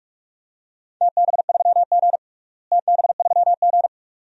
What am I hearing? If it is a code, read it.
Morse code, heard as a keyed tone, about 46 wpm, 700 Hz. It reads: TB3G TB3G
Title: DXCC-100 Call Signs - 4 Characters - Repeated 2x - 45wpm